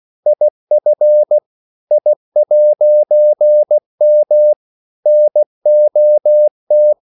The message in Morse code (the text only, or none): IF I'M NOT